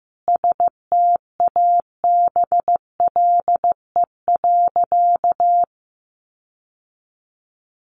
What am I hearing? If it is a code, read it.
Morse code, heard as a keyed tone, about 15 wpm, 700 Hz. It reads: STABLE.
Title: Sentences from Top 1500 Words 15wpm